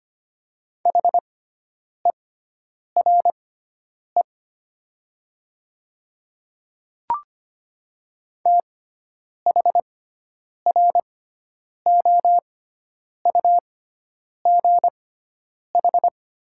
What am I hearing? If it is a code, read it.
Morse code, heard as a keyed tone, about 25 wpm, spaced as for 9 wpm, 700 Hz. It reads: HERE THROUGH
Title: 6x Character Spacing - Top 300 Words - 25wpm